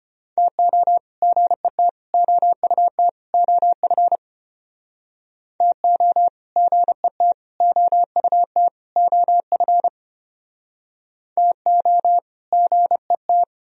Morse code, heard as a keyed tone, about 31 wpm, 700 Hz. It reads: TO GET OUT OF TO GET OUT OF TO GET
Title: Speed-Racing - Sets of 4 Words 25wpm